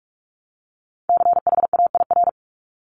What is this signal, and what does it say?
Morse code, heard as a keyed tone, about 46 wpm, 700 Hz. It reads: CHAIR
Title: Multi-letter Phonemes 45wpm